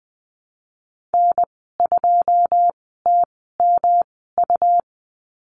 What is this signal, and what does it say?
Morse code, heard as a keyed tone, about 20 wpm, spaced as for 14 wpm, 700 Hz. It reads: N2TMU
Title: 161 Lesson 26g - Letter g - Call Signs 20wpm